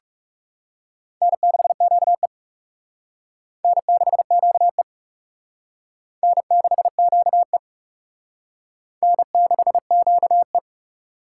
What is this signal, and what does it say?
Morse code, heard as a keyed tone, about 38 wpm, 700 Hz. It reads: N6QE N6QE N6QE N6QE